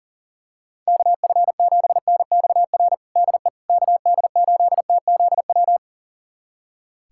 Morse code, heard as a keyed tone, about 40 wpm, 700 Hz. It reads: KF7NXR DE KD8TZW
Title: USA Callsign DE Callsign - 6 Characters - 40wpm